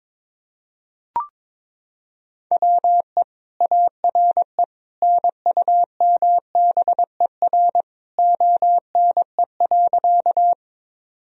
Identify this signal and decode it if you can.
Morse code, heard as a keyed tone, about 21 wpm, 700 Hz. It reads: WE ARE NUMBER ONE.
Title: Sentences from Top 1000 Words - Review Part 1 - 22wpm